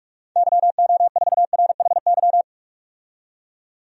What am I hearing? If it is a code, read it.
Morse code, heard as a keyed tone, about 46 wpm, 700 Hz. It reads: YO4RHY
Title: DXCC-100 Call Signs - 6 Characters - 45wpm